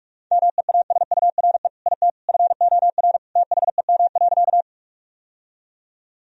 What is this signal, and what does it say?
Morse code, heard as a keyed tone, about 46 wpm, 700 Hz. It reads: MEASURE IT FOR THEM.
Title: Sentences from Top 1000 Words - Review Part 2 - 45wpm